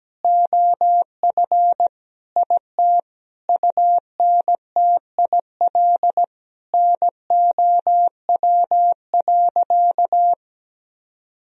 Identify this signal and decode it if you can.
Morse code, heard as a keyed tone, about 17 wpm, 700 Hz. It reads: OF IT UNTIL NOW.
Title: Sentences from Top 1000 Words - Review Part 1 - 17wpm